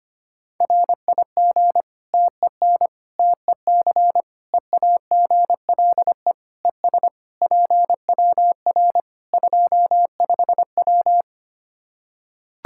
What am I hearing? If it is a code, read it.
Morse code, heard as a keyed tone, about 25 wpm, 700 Hz. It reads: RIG TEN TEC EAGLE ES PWR 25W
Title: K1USN SST QSO Practice 25wpm